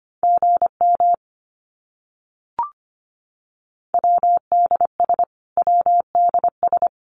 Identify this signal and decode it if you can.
Morse code, heard as a keyed tone, about 25 wpm, 700 Hz. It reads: GM WDS WDS
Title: Abbreviations Repeated 3x - 25wpm